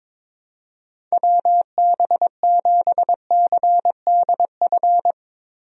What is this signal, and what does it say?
Morse code, heard as a keyed tone, about 22 wpm, 700 Hz. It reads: WB7CDF
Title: USA Call Signs - 6 Characters - Repeated 3x - 22wpm